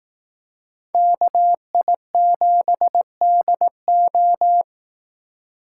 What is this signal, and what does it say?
Morse code, heard as a keyed tone, about 18 wpm, 700 Hz. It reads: KI7DO
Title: Speed Racing - USA 2x2 Call Signs 15wpm